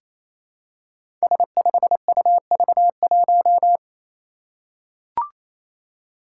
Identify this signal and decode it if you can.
Morse code, heard as a keyed tone, about 27 wpm, 700 Hz. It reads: S5UV1
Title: Five Challenging Letter-Number Combinations 28wpm